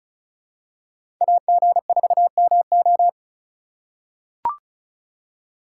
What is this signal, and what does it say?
Morse code, heard as a keyed tone, about 36 wpm, 700 Hz. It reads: AG4MO
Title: USA Call Signs - 5 Characters - 3x Character Spacing - 35wpm